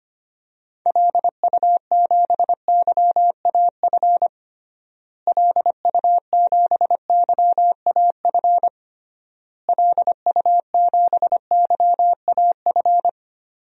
Morse code, heard as a keyed tone, about 25 wpm, 700 Hz. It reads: LU7YAF LU7YAF LU7YAF
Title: DXCC-100 Call Signs - 6 Characters - Repeated 3x - 25wpm